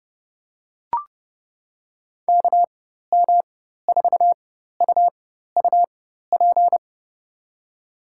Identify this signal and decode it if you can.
Morse code, heard as a keyed tone, about 29 wpm, spaced as for 14 wpm, 700 Hz. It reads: KM4UUP
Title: USA Call Signs - 6 Characters - 4x Character Spacing - 30wpm